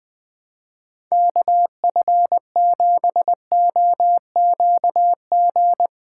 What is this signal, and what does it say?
Morse code, heard as a keyed tone, about 21 wpm, 700 Hz. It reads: KF7OQG
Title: USA Callsign DE Callsign - 6 Characters - 20wpm